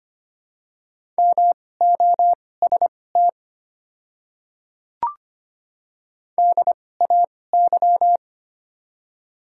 Morse code, heard as a keyed tone, about 25 wpm, spaced as for 18 wpm, 700 Hz. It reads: MOST DAY
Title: 2x Character Spacing - Top 100 Words - 25wpm